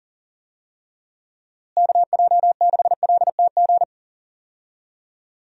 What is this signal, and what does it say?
Morse code, heard as a keyed tone, about 40 wpm, 700 Hz. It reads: KJ6LTG
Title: USA Call Signs - 6 Characters - Repeated 2x - 40wpm